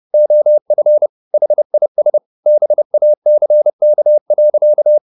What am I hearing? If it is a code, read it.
Morse code, heard as a keyed tone, about 30 wpm, 600 Hz. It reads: OF HIS BACK.
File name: the old man and the sea / the old man and the sea 1 30wpm